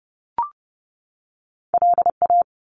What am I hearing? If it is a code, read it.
Morse code, heard as a keyed tone, about 30 wpm, 700 Hz. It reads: LA
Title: US States 30wpm